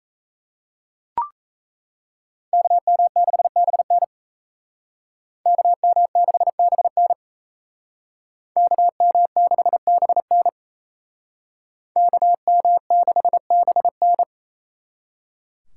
Morse code, heard as a keyed tone, about 35 wpm, 700 Hz. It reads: KM6BN KM6BN KM6BN KM6BN
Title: Speed Racing - USA 2x2 Call Signs 28wpm